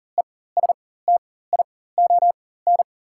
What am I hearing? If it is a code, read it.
Morse code, heard as a keyed tone, about 37 wpm, spaced as for 18 wpm, 700 Hz. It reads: ESTION
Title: 4x Character Spacing - Top 500 Words - 40wpm